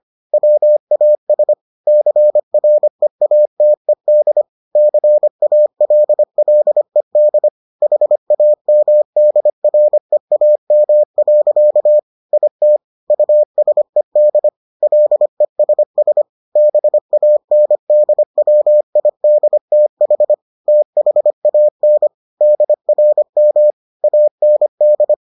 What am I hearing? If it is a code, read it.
Morse code, heard as a keyed tone, about 25 wpm, 600 Hz. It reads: WAS CREATED CALLED HAMDREAM. IT USED LESS BANDWIDTH THAN DRM AND